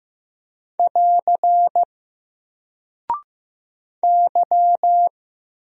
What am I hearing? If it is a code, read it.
Morse code, heard as a keyed tone, about 15 wpm, 700 Hz. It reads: <AR> Y
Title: Single Character (Letter:Number:Punctuation:Prosign) Alt 15wpm